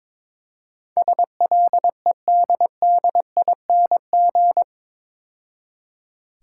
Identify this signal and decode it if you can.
Morse code, heard as a keyed tone, about 24 wpm, 700 Hz. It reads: SLEDDING
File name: Easy Words - Speed-Racing with Hints - 22wpm